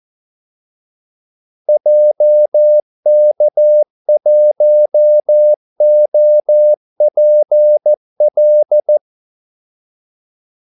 Morse code, heard as a keyed tone, about 14 wpm, 600 Hz. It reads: JK1OPL
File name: Callsigns-14wpm